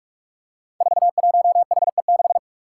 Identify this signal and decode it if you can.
Morse code, heard as a keyed tone, about 43 wpm, 700 Hz. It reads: 41HE6
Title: Five Challenging Letter-Number Combinations 45wpm